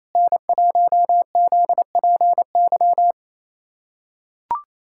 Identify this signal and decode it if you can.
Morse code, heard as a keyed tone, about 27 wpm, 700 Hz. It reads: N1ZPY
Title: USA Callsign DE Callsign - 5 Characters - 28wpm